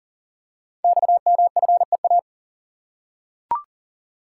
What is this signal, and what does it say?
Morse code, heard as a keyed tone, about 40 wpm, 690 Hz. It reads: XMFEA
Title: Code Groups - 5 Letter:Numbers - 40wpm